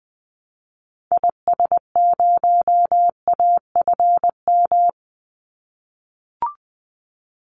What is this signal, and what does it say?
Morse code, heard as a keyed tone, about 20 wpm, 700 Hz. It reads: IS0AFM
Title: Call Signs - 20wpm